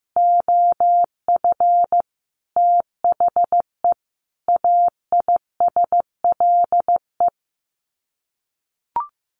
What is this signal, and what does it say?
Morse code, heard as a keyed tone, about 15 wpm, 700 Hz. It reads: OF THE AISLE
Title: Sets of 5 Words Encore2 15wpm